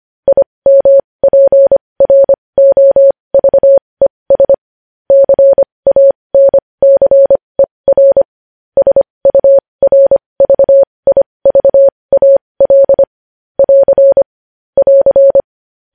Morse code, heard as a keyed tone, about 25 wpm, spaced as for 20 wpm, 550 Hz. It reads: IMPROVES CANCER SURVIVAL <AR> <AR>
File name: podcast / News headlines-1x-25x20